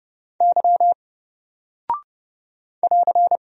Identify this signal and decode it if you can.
Morse code, heard as a keyed tone, about 30 wpm, 700 Hz. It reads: Y <AR>
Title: Single Character (Letter:Number:Punctuation:Prosign) Alt 30wpm